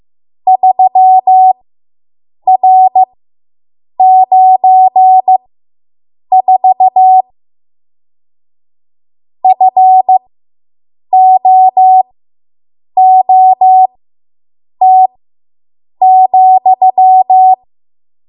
Morse code, heard as a keyed tone, about 15 wpm, spaced as for 7 wpm, 750 Hz. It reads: 3R94 FOOT,